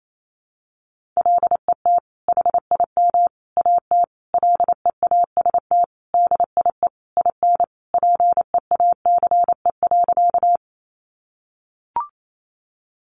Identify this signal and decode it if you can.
Morse code, heard as a keyed tone, about 28 wpm, 700 Hz. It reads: LET HIM AT LEAST DIE IN PEACE.
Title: Sentences from Top 1500 Words 28wpm